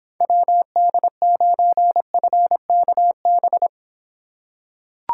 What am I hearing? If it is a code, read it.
Morse code, heard as a keyed tone, about 26 wpm, 700 Hz. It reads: WD9FKB